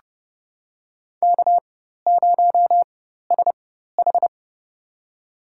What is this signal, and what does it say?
Morse code, heard as a keyed tone, about 30 wpm, spaced as for 14 wpm, 700 Hz. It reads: K0SH
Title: USA Call Signs - 4 Characters - 4x Character Spacing - 30wpm